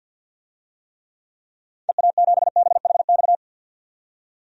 Morse code, heard as a keyed tone, about 47 wpm, 700 Hz. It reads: EA7BHX
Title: DXCC-100 Call Signs - 6 Characters - 50wpm